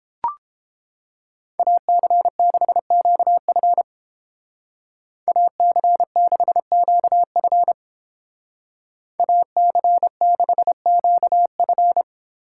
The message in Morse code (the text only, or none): AC6QF AC6QF AC6QF